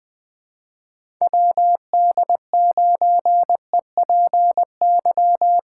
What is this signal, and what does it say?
Morse code, heard as a keyed tone, about 20 wpm, 700 Hz. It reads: WD9EPY
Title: USA Callsign DE Callsign 20wpm